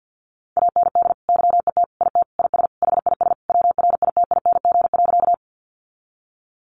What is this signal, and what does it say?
Morse code, heard as a keyed tone, about 50 wpm, 700 Hz. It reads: AND YET IT IS HIS WRITING.
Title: Sentences from Top 800 Words 50wpm